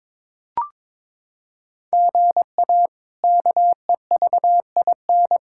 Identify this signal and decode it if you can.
Morse code, heard as a keyed tone, about 22 wpm, 700 Hz. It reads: GA KEVIN
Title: K1USN SST QSO Practice - 22wpm